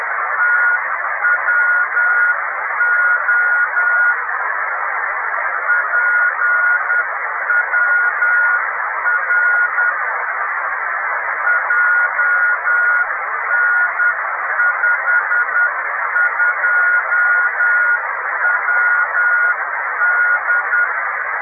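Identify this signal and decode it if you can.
Morse code, heard as a keyed tone, about 10 wpm, 1440 Hz, with radio noise. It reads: TWO PWR JN52WD
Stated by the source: Italian amateur beacon IW0HK on 28322 kHz.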